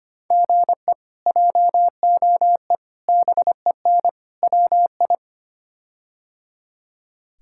Morse code, heard as a keyed tone, about 25 wpm, 700 Hz. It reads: GE JOE BEN WI
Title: K1USN SST QSO Practice - 25wpm